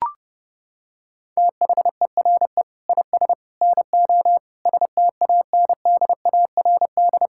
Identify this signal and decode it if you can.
Morse code, heard as a keyed tone, about 30 wpm, 700 Hz. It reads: THERE IS NO STANDARD
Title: Sentences from Top 1600 Words 30wpm